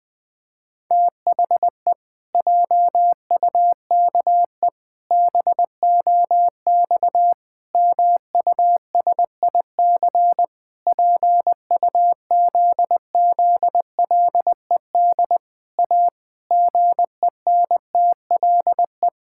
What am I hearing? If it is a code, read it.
Morse code, heard as a keyed tone, about 20 wpm, 700 Hz. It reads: THE JUKE BOX MUSIC PUZZLED A GENTLE
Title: Pangram Challenge v2 - 20wpm